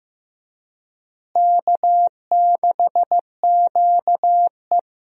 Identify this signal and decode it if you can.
Morse code, heard as a keyed tone, about 15 wpm, 700 Hz. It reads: K6QE
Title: USA Call Signs - 4 Characters - Repeated 3x - 15wpm